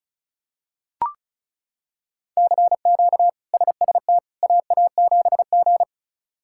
Morse code, heard as a keyed tone, about 34 wpm, 700 Hz. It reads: CQ SST AA7G
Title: K1USN SST QSO Practice - 35wpm